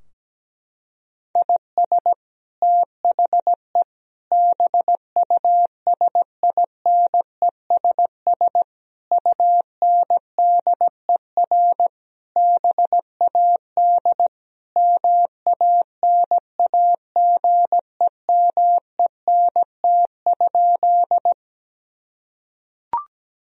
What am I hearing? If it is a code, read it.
Morse code, heard as a keyed tone, about 17 wpm, 700 Hz. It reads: IS THE BUSINESS UNDER BAD MANAGEMENT?
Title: Sentences from Top 1000 Words - Review Part 4 - 17wpm